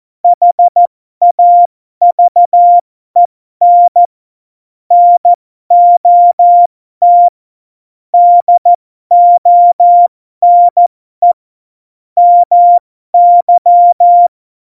Morse code, heard as a keyed tone, about 14 wpm, spaced as for 12 wpm, 700 Hz. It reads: HAVEN NOT DONE MY
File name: Morsekurs / Witze